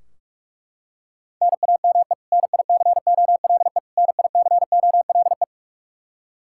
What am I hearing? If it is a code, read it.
Morse code, heard as a keyed tone, about 41 wpm, 700 Hz. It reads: NAME NICOLE NICOLE
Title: Practice QSO - Exchange 1 - Beginning - 45wpm